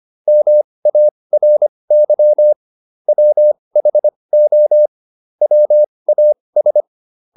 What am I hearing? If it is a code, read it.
Morse code, heard as a keyed tone, about 25 wpm, spaced as for 20 wpm, 600 Hz. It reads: MARY WHO WAS